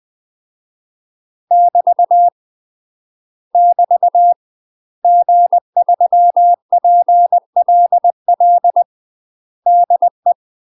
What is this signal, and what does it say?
Morse code, heard as a keyed tone, about 20 wpm, 700 Hz. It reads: <BT> <BT> G3PLL DE
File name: UK MASTER QSOs TRAINING / 20WPM / UK MASTER QSOs 0061